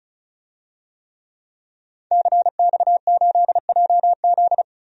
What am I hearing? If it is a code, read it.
Morse code, heard as a keyed tone, about 34 wpm, 700 Hz. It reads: CX8JZ
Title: DXCC-100 Call Signs - 5 Characters - 35wpm